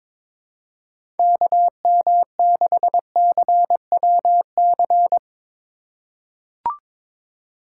Morse code, heard as a keyed tone, about 22 wpm, 700 Hz. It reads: KM6CWC
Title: USA Call Signs - 6 Characters - 4x Character Spacing - 22wpm